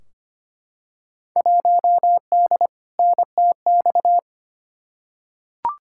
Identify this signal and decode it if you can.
Morse code, heard as a keyed tone, about 25 wpm, 700 Hz. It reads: 1D NTX
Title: Field Day Exchanges - 25wpm